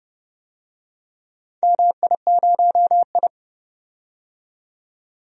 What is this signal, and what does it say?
Morse code, heard as a keyed tone, about 30 wpm, 700 Hz. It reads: MI0I
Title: DXCC-100 Call Signs - 4 Characters - 30wpm